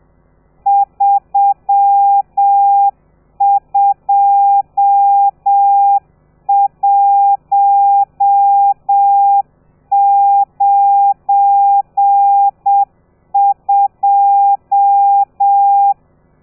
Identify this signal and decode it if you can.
Morse code, heard as a keyed tone, about 7 wpm, 800 Hz. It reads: 32192
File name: telegrafia / i2tfj / Lez12 30